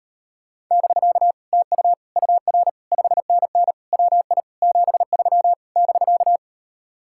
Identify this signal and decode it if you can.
Morse code, heard as a keyed tone, about 38 wpm, 700 Hz. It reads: <BK> TU UR 5NN WI 73 <BK>